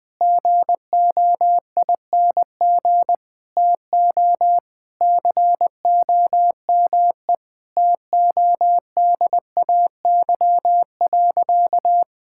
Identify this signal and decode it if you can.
Morse code, heard as a keyed tone, about 20 wpm, 700 Hz. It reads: GOING TO COME TODAY.
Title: Sentences with 3-Word Phrases - Spoken Beginning - 20wpm